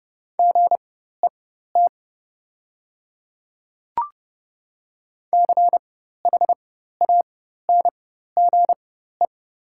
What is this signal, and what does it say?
Morse code, heard as a keyed tone, about 30 wpm, spaced as for 14 wpm, 700 Hz. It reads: GET CHANGE
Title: 4x Character Spacing - Top 200 Words - 30wpm